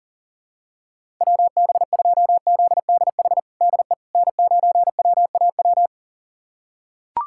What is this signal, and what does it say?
Morse code, heard as a keyed tone, about 40 wpm, 700 Hz. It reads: WB2ZDH DE N9WAW
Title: USA Callsign DE Callsign 40wpm